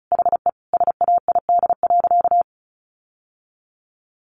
Morse code, heard as a keyed tone, about 34 wpm, 700 Hz. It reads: HE SAID.
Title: Sentences from Top 1300 Words 35wpm